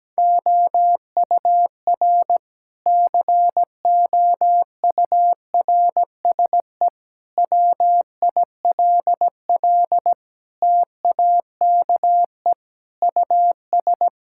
Morse code, heard as a keyed tone, about 17 wpm, 700 Hz. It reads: OUR COURSE WILL TAKE US
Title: Sentences from Top 1000 Words - Review Part 6 - 17wpm